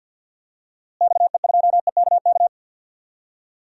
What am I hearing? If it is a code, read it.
Morse code, heard as a keyed tone, about 47 wpm, 700 Hz. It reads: XE2EKK
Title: DXCC-100 Call Signs - 6 Characters - Repeated 2x - 50wpm